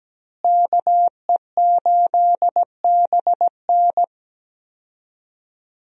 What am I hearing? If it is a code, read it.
Morse code, heard as a keyed tone, about 17 wpm, 700 Hz. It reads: KE8BN
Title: USA Call Signs - 5 Characters - Repeated 3x - 17wpm